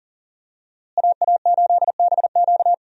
Morse code, heard as a keyed tone, about 40 wpm, 700 Hz. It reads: AA8BQ
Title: USA Call Signs - 5 Characters - Repeated 3x - 40wpm